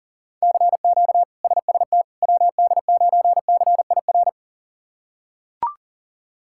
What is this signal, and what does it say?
Morse code, heard as a keyed tone, about 40 wpm, 700 Hz. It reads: CQ SST WD9CIR
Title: K1USN SST QSO Practice - 40wpm